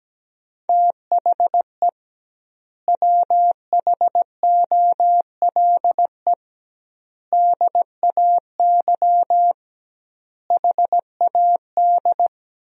Morse code, heard as a keyed tone, about 17 wpm, 700 Hz. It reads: THE WHOLE DAY HAD
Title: 2x Word-Spacing - Sentences from Top 500 Words - 17wpm